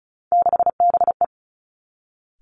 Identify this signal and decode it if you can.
Morse code, heard as a keyed tone, about 34 wpm, 700 Hz. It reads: 6BE
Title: Three Challenging Letter-Number Combinations 35wpm